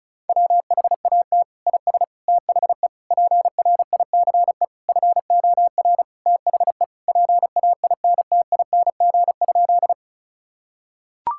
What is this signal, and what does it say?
Morse code, heard as a keyed tone, about 36 wpm, 700 Hz. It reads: WHAT IS THE PRICE FOR THE PAINTING?
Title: Sentences from Top 1000 Words - Review Part 6 - 35wpm